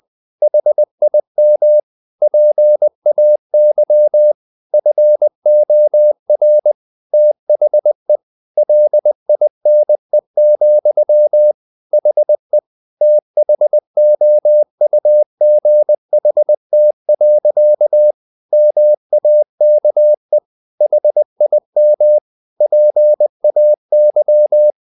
Morse code, heard as a keyed tone, about 20 wpm, 600 Hz. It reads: HIM PAY FOR THE LINE, HE THOUGHT. MAKE HIM PAY
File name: the old man and the sea part 2 20wpm